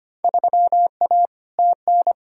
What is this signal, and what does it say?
Morse code, heard as a keyed tone, about 25 wpm, 700 Hz. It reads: 3A TN
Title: Field Day Exchanges - 25wpm